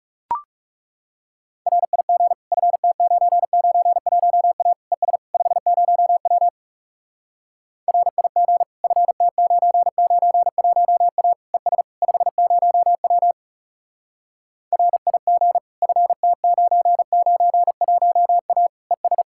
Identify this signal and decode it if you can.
Morse code, heard as a keyed tone, about 40 wpm, 700 Hz. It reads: RIG FT991A ES 50W RIG FT991A ES 50W RIG FT991A ES